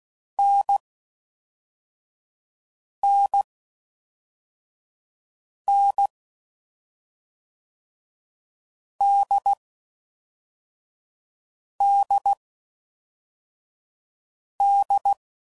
Morse code, heard as a keyed tone, about 16 wpm, spaced as for 4 wpm, 800 Hz. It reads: NNN DDD
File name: Combinations / Difficult Combos / NDB NDB6